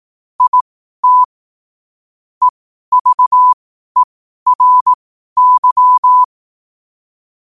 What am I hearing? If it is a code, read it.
Morse code, heard as a keyed tone, about 18 wpm, spaced as for 12 wpm, 1000 Hz. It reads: IT EVERY
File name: qotdfiles / 12wpm